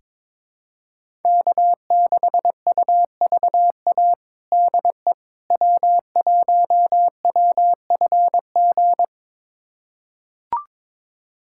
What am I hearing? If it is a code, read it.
Morse code, heard as a keyed tone, about 21 wpm, 700 Hz. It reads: K6UVA DE W1WFG
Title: USA Callsign DE Callsign - 5 Characters - 22wpm